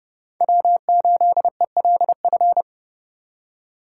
Morse code, heard as a keyed tone, about 28 wpm, 700 Hz. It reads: W8ELF